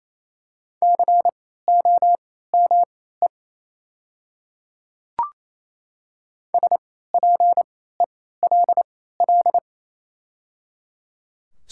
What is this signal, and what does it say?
Morse code, heard as a keyed tone, about 28 wpm, spaced as for 16 wpm, 700 Hz. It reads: COME SPELL
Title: 3x Character Spacing - Top 300 Words - 28wpm